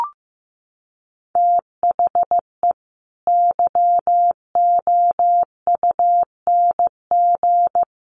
Morse code, heard as a keyed tone, about 15 wpm, 700 Hz. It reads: THE YOUNG
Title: Sentences from Top 1100 Words 15wpm